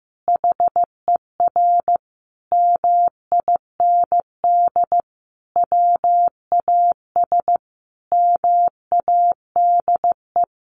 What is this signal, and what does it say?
Morse code, heard as a keyed tone, about 15 wpm, 700 Hz. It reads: HER MIND WAS MADE
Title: Sentences from Top 1300 Words 15wpm